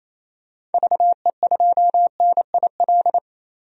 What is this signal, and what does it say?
Morse code, heard as a keyed tone, about 28 wpm, 700 Hz. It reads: VE2NIL
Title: DXCC-100 Call Signs - 6 Characters - 28wpm